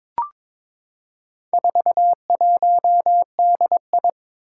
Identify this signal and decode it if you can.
Morse code, heard as a keyed tone, about 22 wpm, 700 Hz. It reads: 41DI
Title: Four Challenging Letter-Number Combinations 22wpm